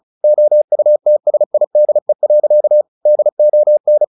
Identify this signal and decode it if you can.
Morse code, heard as a keyed tone, about 34 wpm, 600 Hz. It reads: OUTSIDE. DON
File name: the old man and the sea 2 35wpm